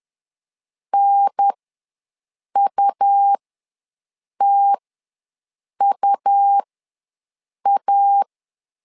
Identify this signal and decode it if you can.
Morse code, heard as a keyed tone, about 10 wpm, spaced as for 6 wpm, 800 Hz. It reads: NUTUA
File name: 2 UHDN ws 10wpm